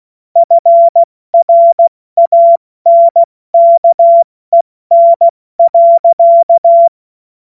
Morse code, heard as a keyed tone, about 16 wpm, spaced as for 14 wpm, 680 Hz. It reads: FRANKEN.
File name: Morsekurs / Witze / Witz 31. August 2021, 70 F80